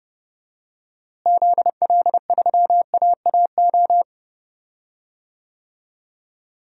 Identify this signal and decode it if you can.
Morse code, heard as a keyed tone, about 31 wpm, 700 Hz. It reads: ZL3AAO
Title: DXCC-100 Call Signs - 6 Characters - Repeated 2x - 30wpm